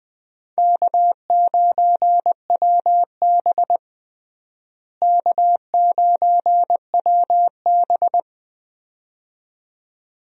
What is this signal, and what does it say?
Morse code, heard as a keyed tone, about 20 wpm, 700 Hz. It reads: K9WB K9WB
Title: USA Call Signs - 4 Characters - Repeated 2x - 20wpm